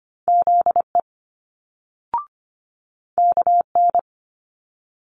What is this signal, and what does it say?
Morse code, heard as a keyed tone, about 25 wpm, 700 Hz. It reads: ZE KN
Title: Multi-letter Phonemes Quick Fire 25wpm